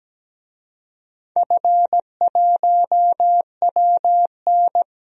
Morse code, heard as a keyed tone, about 17 wpm, 700 Hz. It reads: F1WN
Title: DXCC-100 Call Signs - 4 Characters - 17wpm